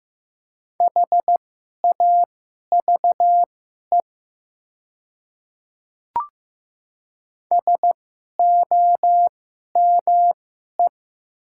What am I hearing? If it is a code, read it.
Morse code, heard as a keyed tone, about 15 wpm, spaced as for 11 wpm, 700 Hz. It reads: HAVE SOME
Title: 2x Character Spacing - Top 100 Words - 15wpm